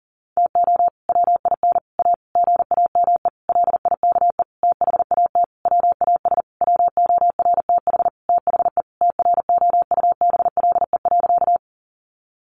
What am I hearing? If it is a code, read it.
Morse code, heard as a keyed tone, about 40 wpm, 700 Hz. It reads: TO WIN A GAME LIKE THAT WAS WORTH THE TROUBLE.
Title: Sentences from Top 1500 Words 40wpm